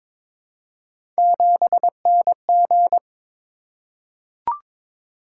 Morse code, heard as a keyed tone, about 22 wpm, 700 Hz. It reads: 7NG
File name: practice / 22wpm / Three Letter-Number Combinations - 22wpm